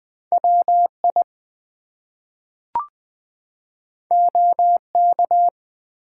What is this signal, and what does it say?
Morse code, heard as a keyed tone, about 20 wpm, 700 Hz. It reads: WI OK
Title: US States and Canadian Provinces 20wpm